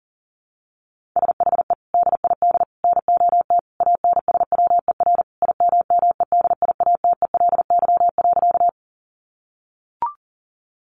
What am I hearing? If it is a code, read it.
Morse code, heard as a keyed tone, about 40 wpm, 700 Hz. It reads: SHE DID NOT ANSWER IMMEDIATELY.
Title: Sentences from Top 1100 Words 40wpm